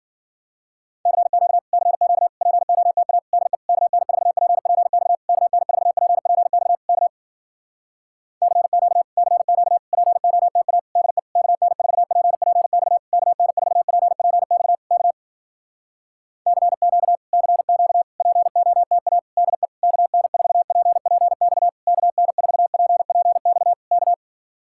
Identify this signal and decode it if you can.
Morse code, heard as a keyed tone, about 58 wpm, 700 Hz. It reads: CQ CQ POTA DE KN4PPX KN4PPX K CQ CQ POTA DE KN4PPX KN4PPX K CQ CQ POTA DE KN4PPX KN4PPX K